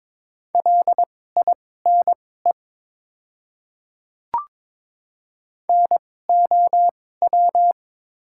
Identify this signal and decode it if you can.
Morse code, heard as a keyed tone, about 21 wpm, spaced as for 16 wpm, 700 Hz. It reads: LINE NOW
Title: 2x Character Spacing - Top 200 Words - 22wpm